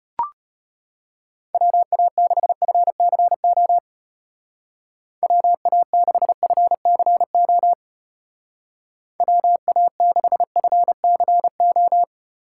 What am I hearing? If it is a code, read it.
Morse code, heard as a keyed tone, about 33 wpm, 700 Hz. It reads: WA6FCO WA6FCO WA6FCO